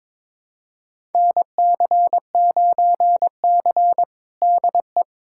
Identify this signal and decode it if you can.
Morse code, heard as a keyed tone, about 22 wpm, 700 Hz. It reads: NC9C DE
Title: USA Callsign DE Callsign - 4 Characters - 22wpm